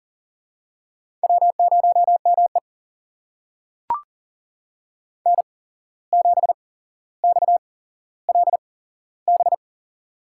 Morse code, heard as a keyed tone, about 40 wpm, spaced as for 16 wpm, 700 Hz. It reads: W0ME N7XLB
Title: USA Call Signs - 8x Character Spacing - 40wpm